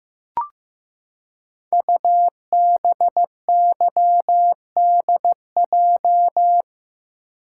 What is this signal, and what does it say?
Morse code, heard as a keyed tone, about 15 wpm, 700 Hz. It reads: UBYDJ
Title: Code Groups - 5 Letters - 15wpm